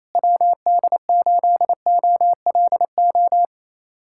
Morse code, heard as a keyed tone, about 27 wpm, 700 Hz. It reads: WD8OLO
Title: USA Callsign DE Callsign - 6 Characters - 28wpm